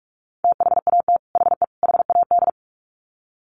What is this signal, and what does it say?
Morse code, heard as a keyed tone, about 44 wpm, 700 Hz. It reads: THAT HE HAD
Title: Sets of 3 Words 45wpm